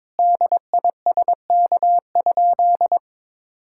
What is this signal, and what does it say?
Morse code, heard as a keyed tone, about 21 wpm, 700 Hz. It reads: DISK?
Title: Sentences from Top 1000 Words - Review Part 6 - 22wpm